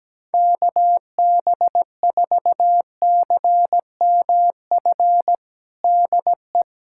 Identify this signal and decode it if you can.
Morse code, heard as a keyed tone, about 17 wpm, 700 Hz. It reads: KB4CMF DE
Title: USA Callsign DE Callsign 17wpm